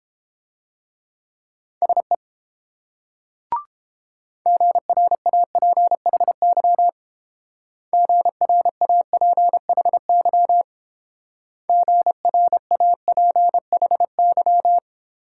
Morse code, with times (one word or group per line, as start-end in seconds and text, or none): 1.8-2.2 SE
4.5-6.9 GRAPHY
7.9-10.6 GRAPHY
11.7-14.8 GRAPHY